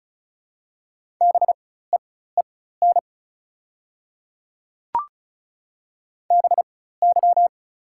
Morse code, heard as a keyed tone, about 34 wpm, spaced as for 16 wpm, 700 Hz. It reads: BEEN BY
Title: 4x Character Spacing - Top 100 Words - 35wpm